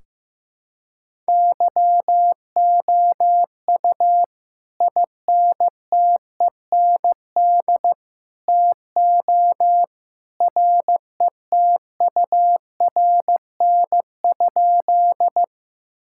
Morse code, heard as a keyed tone, about 16 wpm, 700 Hz. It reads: YOU INTEND TO RETURN?
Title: Sentences from Top 1600 Words 15wpm